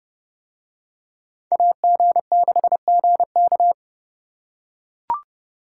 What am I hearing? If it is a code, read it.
Morse code, heard as a keyed tone, about 29 wpm, 700 Hz. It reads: AG6GK